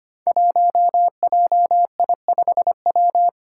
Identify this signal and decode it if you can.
Morse code, heard as a keyed tone, about 25 wpm, 700 Hz. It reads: 1JI5W
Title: Five Challenging Letter-Number Combinations 25wpm